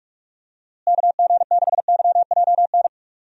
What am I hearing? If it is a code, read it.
Morse code, heard as a keyed tone, about 46 wpm, 700 Hz. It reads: KG6YJN
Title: USA Call Signs - 6 Characters - Repeated 2x - 45wpm